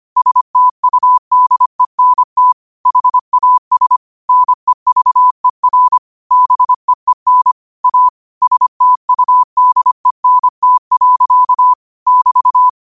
Morse code, heard as a keyed tone, about 25 wpm, 1000 Hz. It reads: STUDENT HAS NEVER BEEN A STUDENT. <BT>
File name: qotdfiles / 25wpm